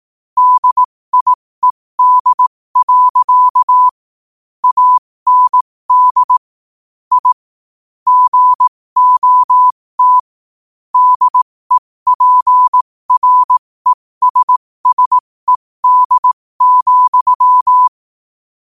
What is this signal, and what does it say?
Morse code, heard as a keyed tone, about 18 wpm, spaced as for 15 wpm, 1000 Hz. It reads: DIED. AND I GOT DEPRESSED,
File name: qotdfiles / 15wpm